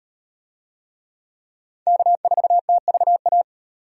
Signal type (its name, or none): Morse code